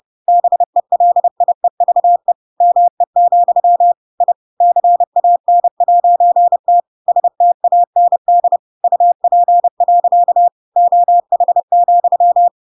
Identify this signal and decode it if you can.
Morse code, heard as a keyed tone, about 30 wpm, 700 Hz. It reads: BELIEVE ME, I CAN'T STAND UP. OH,
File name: Pinocchio / Pinocchio ch7 30 30 ik1hge